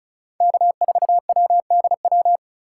Morse code, heard as a keyed tone, about 36 wpm, 700 Hz. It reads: K4WDW
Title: USA Call Signs - 5 Characters - 4x Character Spacing - 35wpm